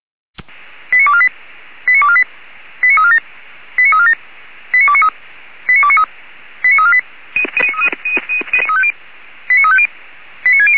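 Five-tone selective call (ZVEI), heard as quick runs of five tones, tones of about 70 ms. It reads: 89248 89248 89347 89347 89293 89293 89348 89350 89609
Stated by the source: ZVEI five-tone selective call 89360 received on SDR.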